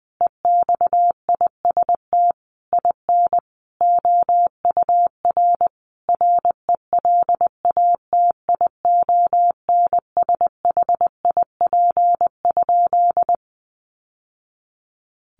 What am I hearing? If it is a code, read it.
Morse code, heard as a keyed tone, about 19 wpm, 700 Hz. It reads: EXIST IN OUR RELATIONSHIP?
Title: Sentences from Top 1100 Words 20wpm